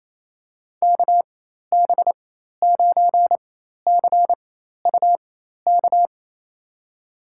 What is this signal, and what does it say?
Morse code, heard as a keyed tone, about 28 wpm, spaced as for 13 wpm, 700 Hz. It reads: KB9CUK
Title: USA Call Signs - 6 Characters - 4x Character Spacing - 28wpm